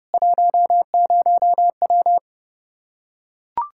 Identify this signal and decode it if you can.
Morse code, heard as a keyed tone, about 32 wpm, 700 Hz. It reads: 10W
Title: Practice QSO - Exchange 2 - Info - 30wpm